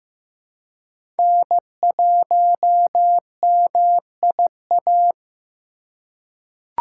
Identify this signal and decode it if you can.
Morse code, heard as a keyed tone, about 15 wpm, 700 Hz. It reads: N1MIA
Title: USA Call Signs - 5 Characters - 4x Character Spacing - 15wpm